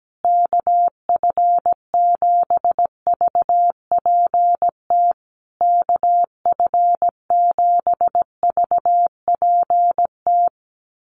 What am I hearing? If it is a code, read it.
Morse code, heard as a keyed tone, about 17 wpm, 700 Hz. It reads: KF7VPT KF7VPT
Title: USA Call Signs Repeated 3x 17wpm